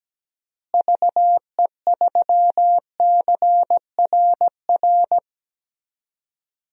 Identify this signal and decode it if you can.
Morse code, heard as a keyed tone, about 17 wpm, 700 Hz. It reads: VE3CRR
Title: DXCC-100 Call Signs - 6 Characters - Repeated 3x - 17wpm